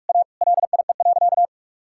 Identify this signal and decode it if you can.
Morse code, heard as a keyed tone, about 46 wpm, 700 Hz. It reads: A LIE.
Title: Sentences from Top 1600 Words 45wpm